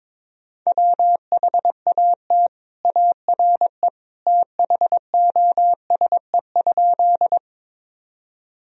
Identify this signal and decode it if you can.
Morse code, heard as a keyed tone, about 22 wpm, 700 Hz. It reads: WHAT ARE THOSE?
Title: Sentences from Top 1000 Words - Review Part 1 - 22wpm